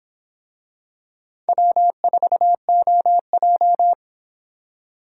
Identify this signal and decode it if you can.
Morse code, heard as a keyed tone, about 26 wpm, 700 Hz. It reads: W4OJ